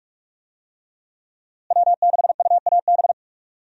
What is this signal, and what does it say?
Morse code, heard as a keyed tone, about 42 wpm, 700 Hz. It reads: W6UAB
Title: USA Call Signs - 5 Characters - Repeated 2x - 45wpm